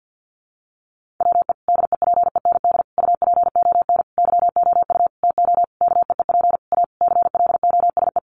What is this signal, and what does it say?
Morse code, heard as a keyed tone, about 50 wpm, 700 Hz. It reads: WE DEPEND UPON YOU TO KEEP A CLOSE
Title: Sentences from Top 1300 Words 50wpm